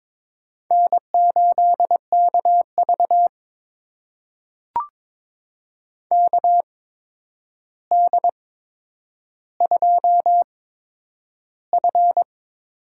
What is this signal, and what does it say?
Morse code, heard as a keyed tone, about 22 wpm, spaced as for 9 wpm, 700 Hz. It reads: N8KV KD2F
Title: USA Call Signs - 8x Character Spacing - 22wpm